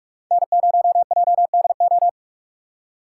Morse code, heard as a keyed tone, about 45 wpm, 700 Hz. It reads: N0JDO